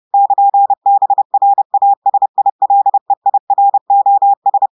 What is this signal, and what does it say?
Morse code, heard as a keyed tone, about 30 wpm, 800 Hz. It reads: <KN> BRASILEIROS
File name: audios / portugais / portugais texte en clair 1 vitesse 30 mots minutes